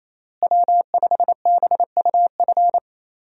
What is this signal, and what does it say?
Morse code, heard as a keyed tone, about 28 wpm, 700 Hz. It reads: W5BUF
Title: USA Call Signs - 5 Characters - Repeated 3x - 28wpm